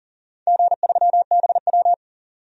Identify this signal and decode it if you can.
Morse code, heard as a keyed tone, about 40 wpm, 700 Hz. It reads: G3BW
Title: DXCC-100 Call Signs - 4 Characters - 40wpm